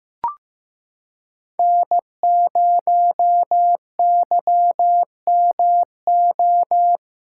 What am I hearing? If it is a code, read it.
Morse code, heard as a keyed tone, about 15 wpm, 700 Hz. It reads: N0YMO
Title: USA Callsign DE Callsign - 5 Characters - 15wpm